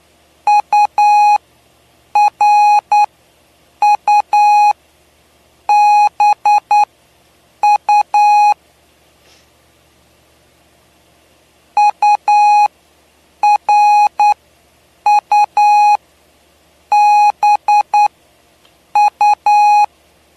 Morse code, heard as a keyed tone, about 9 wpm, spaced as for 7 wpm, 830 Hz. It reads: URUBU URUBU
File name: cw / py2lcd / 07-PY2LCD-Li��o 7